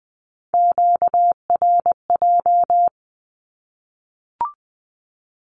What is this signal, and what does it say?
Morse code, heard as a keyed tone, about 20 wpm, 700 Hz. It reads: QRJ
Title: Q-Codes 20wpm